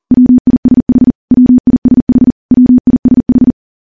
Morse code, heard as a keyed tone, about 40 wpm, 250 Hz. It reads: WISH WISH WISH
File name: morse / morse-training-40WPM